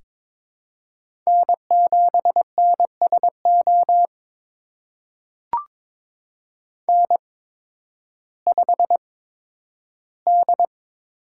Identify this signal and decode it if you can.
Morse code, heard as a keyed tone, about 22 wpm, spaced as for 12 wpm, 700 Hz. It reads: N7NSO N5D
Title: USA Call Signs - 5 Characters - 8x Character Spacing - 22wpm